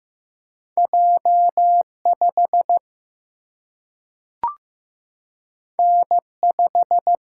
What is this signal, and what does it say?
Morse code, heard as a keyed tone, about 15 wpm, 700 Hz. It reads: J5 N5
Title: Single-Letter Single-Number 15wpm